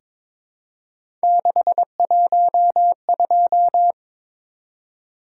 Morse code, heard as a keyed tone, about 22 wpm, 700 Hz. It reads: 612
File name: practice / 22wpm / Three Letter-Number Combinations - 22wpm